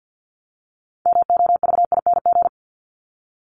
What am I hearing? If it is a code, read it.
Morse code, heard as a keyed tone, about 50 wpm, 700 Hz. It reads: MOVING
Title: ING Suffix - Top 100 Words with Base Words 50wpm